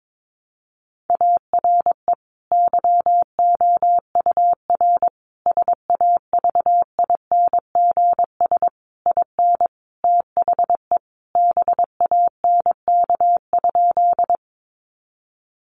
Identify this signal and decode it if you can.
Morse code, heard as a keyed tone, about 22 wpm, 700 Hz. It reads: ARE YOUR SAVINGS IN THE BANK?
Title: Sentences from Top 1200 Words 22wpm